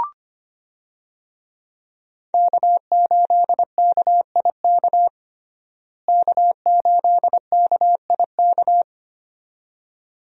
Morse code, heard as a keyed tone, about 25 wpm, 700 Hz. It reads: K8KIK K8KIK
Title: USA Call Signs - 5 Characters - Repeated 2x - 25wpm